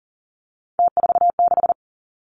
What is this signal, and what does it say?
Morse code, heard as a keyed tone, about 39 wpm, 700 Hz. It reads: T46
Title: Three Challenging Letter-Number Combinations 40wpm